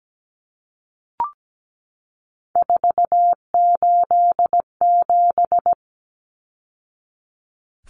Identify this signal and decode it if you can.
Morse code, heard as a keyed tone, about 17 wpm, 700 Hz. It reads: 487
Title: Numbers - 3-Digit 17wpm